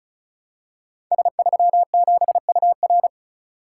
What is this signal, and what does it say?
Morse code, heard as a keyed tone, about 34 wpm, 700 Hz. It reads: S37UR
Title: Code Groups - 5 Letter:Numbers - 35wpm